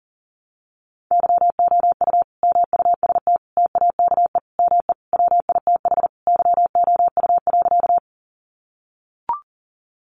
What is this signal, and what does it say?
Morse code, heard as a keyed tone, about 40 wpm, 700 Hz. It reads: YOU MUST TAKE ME WITH YOU.
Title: Sentences from Top 200 Words 40wpm